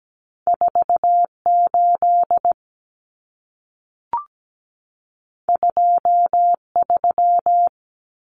Morse code, heard as a keyed tone, about 17 wpm, 700 Hz. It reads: 4 8 2 3
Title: Common Cut Numbers - 2-Digit 17wpm